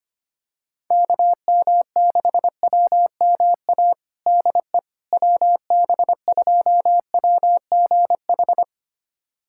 Morse code, heard as a keyed tone, about 25 wpm, 700 Hz. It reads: KM6WMA DE WB2WGH
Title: USA Callsign DE Callsign - 6 Characters - 25wpm